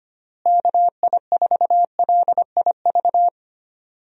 Morse code, heard as a keyed tone, about 25 wpm, 700 Hz. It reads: KI4LIV
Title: USA Call Signs - 6 Characters - Repeated 3x - 25wpm